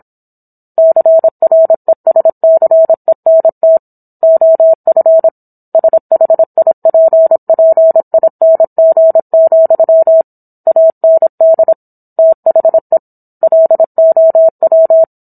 Morse code, heard as a keyed tone, about 27 wpm, 650 Hz. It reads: CRESCENT OF SHIPPING, AND THE LOW